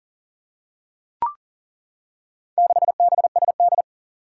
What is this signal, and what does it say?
Morse code, heard as a keyed tone, about 39 wpm, 700 Hz. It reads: 6BSD
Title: Four Challenging Letter-Number Combinations 40wpm